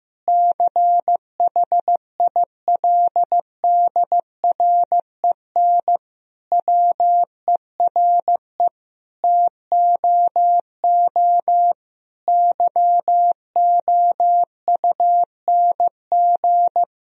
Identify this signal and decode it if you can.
Morse code, heard as a keyed tone, about 16 wpm, 700 Hz. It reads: CHILDREN WERE TOO YOUNG
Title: Sentences from Top 1000 Words - Review Part 4 - 15wpm